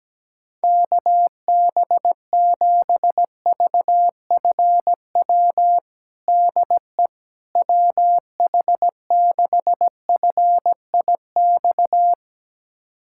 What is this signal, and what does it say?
Morse code, heard as a keyed tone, about 17 wpm, 700 Hz. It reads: KB7VFW DE WH6FIX
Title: USA Callsign DE Callsign - 6 Characters - 17wpm